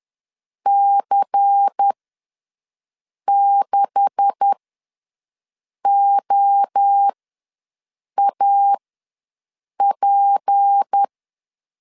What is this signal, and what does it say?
Morse code, heard as a keyed tone, about 10 wpm, spaced as for 5 wpm, 800 Hz. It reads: C6OAP
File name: LEZIONI / 9 286 ws 10wpm 4032